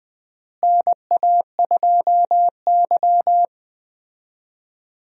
Morse code, heard as a keyed tone, about 20 wpm, 700 Hz. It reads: NA2Y
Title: USA Call Signs - 4 Characters - Repeated 3x - 20wpm